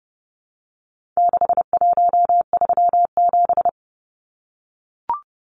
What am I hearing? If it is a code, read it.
Morse code, heard as a keyed tone, about 30 wpm, 700 Hz. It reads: 6137
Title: Numbers - 4-Digit 30wpm